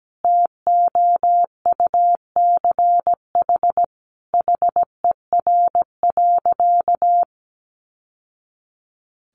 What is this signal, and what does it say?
Morse code, heard as a keyed tone, about 17 wpm, 700 Hz. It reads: TOUCH HER.
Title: Sentences from Top 1200 Words 17wpm